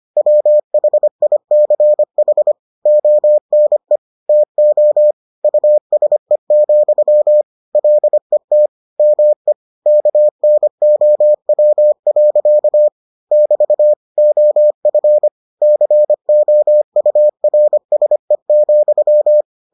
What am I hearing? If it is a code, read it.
Morse code, heard as a keyed tone, about 25 wpm, 600 Hz. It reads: WHICH ONE TO USE, LET ME KNOW. <BT> OF COURSE,